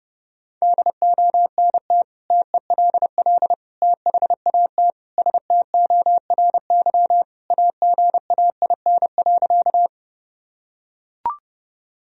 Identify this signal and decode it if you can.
Morse code, heard as a keyed tone, about 30 wpm, 700 Hz. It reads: DONT TELL THAT STORY AGAIN.
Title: Sentences from Top 1000 Words - Review Part 1 - 30wpm